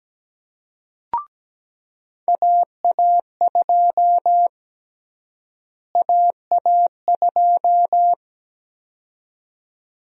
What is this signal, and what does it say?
Morse code, heard as a keyed tone, about 17 wpm, 700 Hz. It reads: AA2 AA2
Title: USA Call Sign Prefixes - Repeated 2x - 17wpm